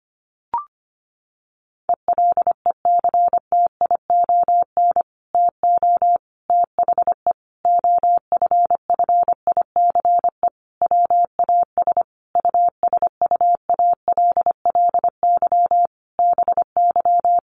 Morse code, heard as a keyed tone, about 25 wpm, 700 Hz. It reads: ELECTION TO THE OFFICE WAS USUALLY BY
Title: Sentences from Top 1500 Words 25wpm